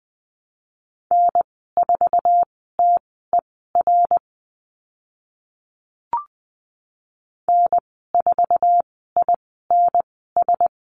Morse code, heard as a keyed tone, about 20 wpm, spaced as for 15 wpm, 700 Hz. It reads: N4TER N4INS
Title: 055 Lesson 10g - Letter r - Call Signs 20wpm